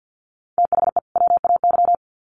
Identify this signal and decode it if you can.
Morse code, heard as a keyed tone, about 50 wpm, 700 Hz. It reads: THE WAY
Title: Sets of 2 Words 50wpm